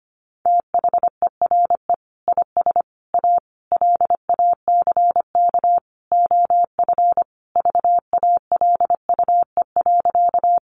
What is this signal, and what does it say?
Morse code, heard as a keyed tone, about 25 wpm, 700 Hz. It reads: THERE IS A LACK OF VALUE.
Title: Sentences from Top 1100 Words 25wpm